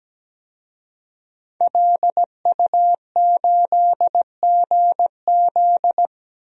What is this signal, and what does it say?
Morse code, heard as a keyed tone, about 17 wpm, 700 Hz. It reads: LU8GZ
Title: DXCC-100 Call Signs - 5 Characters - 17wpm